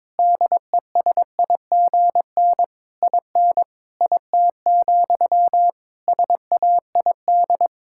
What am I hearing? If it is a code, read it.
Morse code, heard as a keyed tone, about 22 wpm, 700 Hz. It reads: DESIGN IN IT, SAID
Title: Sentences from Top 1000 Words - Review Part 5 - 22wpm